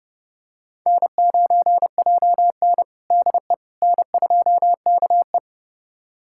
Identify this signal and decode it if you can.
Morse code, heard as a keyed tone, about 30 wpm, 700 Hz. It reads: N9JN DE N2KE
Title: USA Callsign DE Callsign - 4 Characters - 30wpm